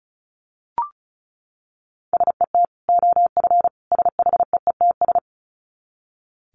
Morse code, heard as a keyed tone, about 34 wpm, 700 Hz. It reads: SET OF SHEETS
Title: Dit Challenge 35wpm